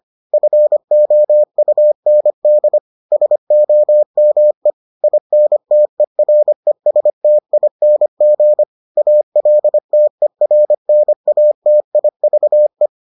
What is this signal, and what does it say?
Morse code, heard as a keyed tone, about 25 wpm, 600 Hz. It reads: FOUND SOME INTERESTING ALTERNATIVE